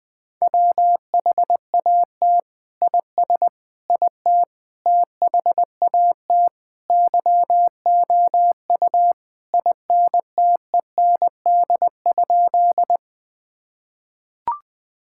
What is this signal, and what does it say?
Morse code, heard as a keyed tone, about 20 wpm, 700 Hz. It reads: WHAT IS IT THAT YOU INTEND?
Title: Sentences from Top 1600 Words 20wpm